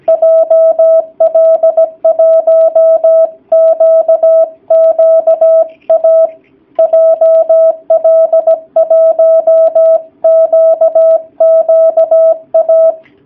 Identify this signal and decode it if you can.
Morse code, heard as a keyed tone, about 17 wpm, 650 Hz. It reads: JL1QQA JL1QQA
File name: jh8cbh / 2019cw / sp16- w